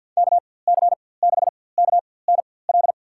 Morse code, heard as a keyed tone, about 52 wpm, spaced as for 24 wpm, 700 Hz. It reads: KC6KNL
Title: USA Call Signs - 6 Characters - 4x Character Spacing - 50wpm